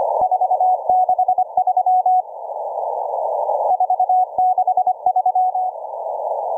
Morse code, heard as a keyed tone, about 25 wpm, 740 Hz. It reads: 463 463